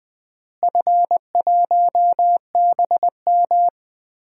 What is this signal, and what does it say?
Morse code, heard as a keyed tone, about 20 wpm, 700 Hz. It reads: F1BM
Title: DXCC-100 Call Signs - 4 Characters - Repeated 2x - 20wpm